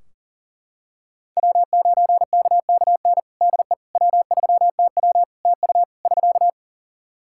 Morse code, heard as a keyed tone, about 37 wpm, 700 Hz. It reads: W9KKN DE W3TW TU <SK>
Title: Practice QSO - Exchange 4 - Ending - 40wpm